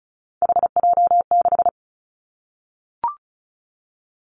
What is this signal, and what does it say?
Morse code, heard as a keyed tone, about 34 wpm, 700 Hz. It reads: HJ6
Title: Callsign Prefixes for Top 100 DXCC - 35wpm